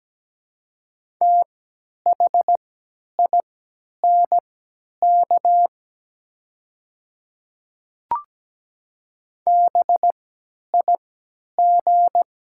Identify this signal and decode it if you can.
Morse code, heard as a keyed tone, about 17 wpm, spaced as for 10 wpm, 700 Hz. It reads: THINK BIG
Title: 3x Character Spacing - Top 500 Words - 17wpm